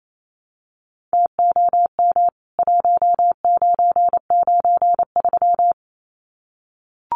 Morse code, heard as a keyed tone, about 28 wpm, 700 Hz. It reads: TOM 1993
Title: CWT Exchange 28wpm